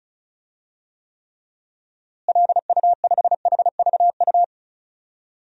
Morse code, heard as a keyed tone, about 36 wpm, 700 Hz. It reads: LU5HVU
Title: DXCC-100 Call Signs - 6 Characters - Repeated 2x - 35wpm